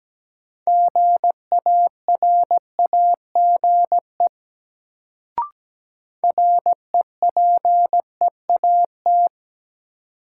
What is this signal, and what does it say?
Morse code, heard as a keyed tone, about 17 wpm, 700 Hz. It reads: GARAGE REPEAT
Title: Sentences from Top 1600 Words - New Word Drill Alt - 17wpm